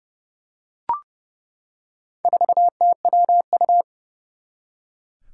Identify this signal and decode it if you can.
Morse code, heard as a keyed tone, about 30 wpm, 700 Hz. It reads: 4TWU
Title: Four Challenging Letter-Number Combinations 30wpm